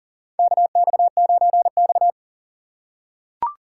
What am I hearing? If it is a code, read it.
Morse code, heard as a keyed tone, about 39 wpm, 700 Hz. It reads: KX9X
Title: USA Call Signs - 4 Characters - 2x Character Spacing - 40wpm